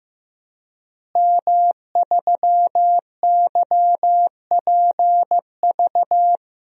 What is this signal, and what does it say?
Morse code, heard as a keyed tone, about 15 wpm, 700 Hz. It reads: M3YPV
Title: DXCC-100 Call Signs - 5 Characters - Repeated 2x - 15wpm